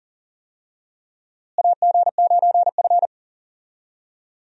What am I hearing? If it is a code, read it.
Morse code, heard as a keyed tone, about 40 wpm, 700 Hz. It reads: AG9F
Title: USA Call Signs - 4 Characters - Repeated 3x - 40wpm